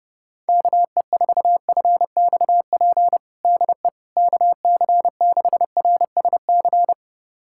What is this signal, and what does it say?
Morse code, heard as a keyed tone, about 30 wpm, 700 Hz. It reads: KE4FXP DE KC6RSC
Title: USA Callsign DE Callsign - 6 Characters - 30wpm